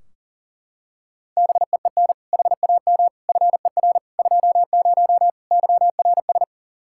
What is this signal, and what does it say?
Morse code, heard as a keyed tone, about 40 wpm, 700 Hz. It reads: BEEN HAM FER 20 YRS
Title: K1USN SST QSO Practice 40wpm